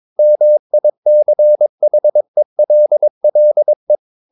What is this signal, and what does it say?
Morse code, heard as a keyed tone, about 22 wpm, 600 Hz. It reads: MICHELLE
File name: Ferie QRS 22wpm